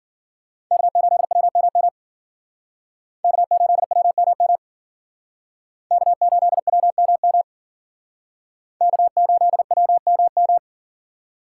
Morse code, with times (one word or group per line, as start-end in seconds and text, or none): 0.7-1.9 K8WMM
3.2-4.6 K8WMM
5.9-7.4 K8WMM
8.8-10.6 K8WMM